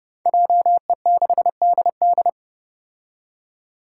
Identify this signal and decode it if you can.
Morse code, heard as a keyed tone, about 28 wpm, 700 Hz. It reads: JE6DD
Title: Five Challenging Letter-Number Combinations 30wpm